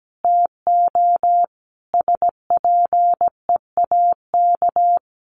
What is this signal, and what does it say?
Morse code, heard as a keyed tone, about 17 wpm, 700 Hz. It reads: TO SPEAK
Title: Sentences from Top 1100 Words 17wpm